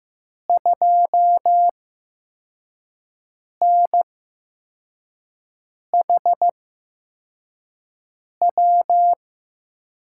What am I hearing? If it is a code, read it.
Morse code, heard as a keyed tone, about 15 wpm, spaced as for 4 wpm, 700 Hz. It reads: 2NHW
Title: USA Call Signs - 8x Character Spacing - 15wpm